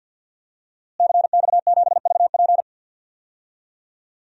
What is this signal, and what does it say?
Morse code, heard as a keyed tone, about 53 wpm, 700 Hz. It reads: CX7VP
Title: DXCC-100 Call Signs - 5 Characters - Repeated 2x - 50wpm